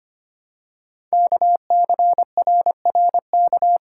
Morse code, heard as a keyed tone, about 25 wpm, 700 Hz. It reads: KCRRK
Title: Code Groups - 5 Letter:Numbers - 25wpm